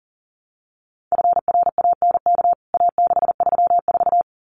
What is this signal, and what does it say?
Morse code, heard as a keyed tone, about 40 wpm, 700 Hz. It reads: FRANK A634
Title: CWT Exchange 40wpm